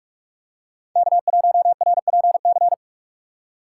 Morse code, heard as a keyed tone, about 46 wpm, 700 Hz. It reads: K1RPC
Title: USA Call Signs - 5 Characters - 3x Character Spacing - 45wpm